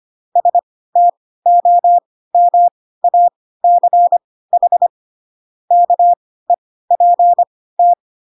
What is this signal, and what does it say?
Morse code, heard as a keyed tone, about 25 wpm, spaced as for 16 wpm, 700 Hz. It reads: STOMACH KEPT